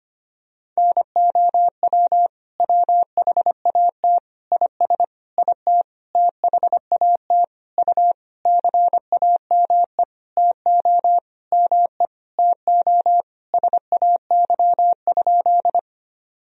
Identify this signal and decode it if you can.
Morse code, heard as a keyed tone, about 25 wpm, 700 Hz. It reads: NOW WHAT IS IT THAT U CAME TO ME TO SAY?
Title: Sentences from Top 200 Words and Abbreviations 25wpm